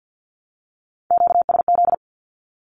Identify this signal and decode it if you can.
Morse code, heard as a keyed tone, about 50 wpm, 700 Hz. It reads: QSZ
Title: Q-Codes 50wpm